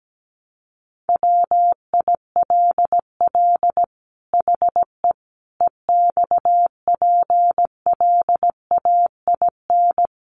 Morse code, heard as a keyed tone, about 17 wpm, 700 Hz. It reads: WILL HE EXPLAIN
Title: Sentences from Top 1100 Words 17wpm